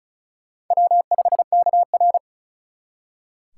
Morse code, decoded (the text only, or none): W5KR